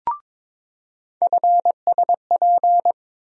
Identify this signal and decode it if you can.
Morse code, heard as a keyed tone, about 21 wpm, 700 Hz. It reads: FSP
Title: USA Call Sign Postfixes - 3 Letters 22wpm